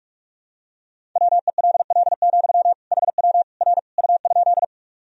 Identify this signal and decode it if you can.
Morse code, heard as a keyed tone, about 46 wpm, 700 Hz. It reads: WELL, HW R U?
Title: Sentences from Top 200 Words and Abbreviations 45wpm